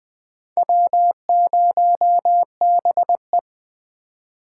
Morse code, heard as a keyed tone, about 20 wpm, 700 Hz. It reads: W0BE
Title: USA Call Signs - 4 Characters - 2x Character Spacing - 20wpm